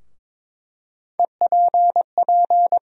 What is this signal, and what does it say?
Morse code, heard as a keyed tone, about 22 wpm, 700 Hz. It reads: EPP
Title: USA Call Sign Postfixes - 3 Letters 22wpm